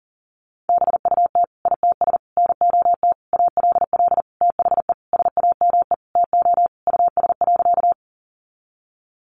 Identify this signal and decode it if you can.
Morse code, heard as a keyed tone, about 40 wpm, 700 Hz. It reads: BUT ITS NOT ALL THE SAME TO US.
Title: Sentences from Top 600 Words 40wpm